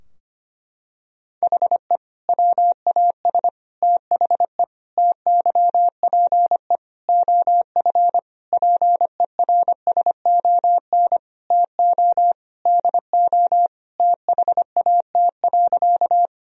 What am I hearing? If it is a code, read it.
Morse code, heard as a keyed tone, about 25 wpm, 700 Hz. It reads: HE WAS THE TYPE OF PERSON TO DO THAT.
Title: Sentences from Top 1000 Words - Review Part 3 - 25wpm